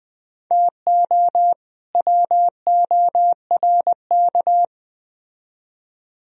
Morse code, heard as a keyed tone, about 20 wpm, 700 Hz. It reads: TO WORK
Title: Speed-Racing - Sets of 2 Words 17wpm